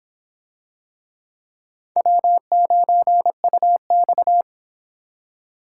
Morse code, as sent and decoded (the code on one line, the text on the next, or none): .-- ----. ..- -..-
W9UX